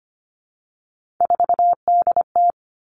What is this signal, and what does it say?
Morse code, heard as a keyed tone, about 25 wpm, 700 Hz. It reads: 4DT
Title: All Cut Numbers - 3-Digit 25wpm